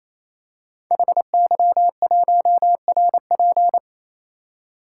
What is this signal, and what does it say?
Morse code, heard as a keyed tone, about 27 wpm, 700 Hz. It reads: HY1RP
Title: Code Groups - 5 Letter:Numbers - 28wpm